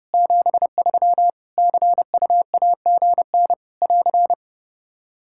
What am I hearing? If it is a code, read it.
Morse code, heard as a keyed tone, about 30 wpm, 700 Hz. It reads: 73 CUAGN <AR>
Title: K1USN SST QSO Practice 30wpm